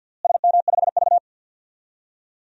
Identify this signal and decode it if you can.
Morse code, heard as a keyed tone, about 50 wpm, 700 Hz. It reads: SM5V
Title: DXCC-100 Call Signs - 4 Characters - Repeated 2x - 50wpm